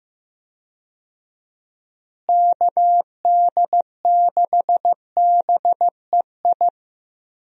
Morse code, heard as a keyed tone, about 15 wpm, 700 Hz. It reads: KD6BEI
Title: USA Call Signs - 6 Characters - Repeated 3x - 15wpm